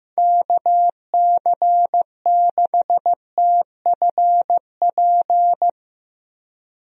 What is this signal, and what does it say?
Morse code, heard as a keyed tone, about 15 wpm, 700 Hz. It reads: KC6TFP
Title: USA Call Signs - 6 Characters - 15wpm